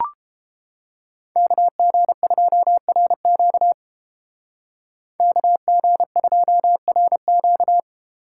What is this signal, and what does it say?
Morse code, heard as a keyed tone, about 30 wpm, 700 Hz. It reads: KG2RQ KG2RQ